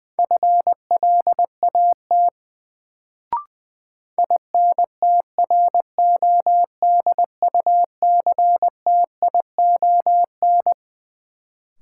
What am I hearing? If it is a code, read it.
Morse code, heard as a keyed tone, about 20 wpm, 700 Hz. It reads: FLAT INTRODUCTION
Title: Sentences from Top 1000 Words - New Word Drill - 20wpm